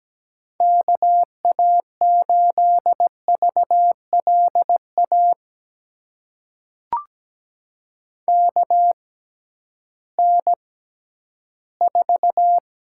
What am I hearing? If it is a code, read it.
Morse code, heard as a keyed tone, about 17 wpm, spaced as for 11 wpm, 700 Hz. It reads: KA8VLA KN4
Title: USA Call Signs - 6x Character Spacing - 17wpm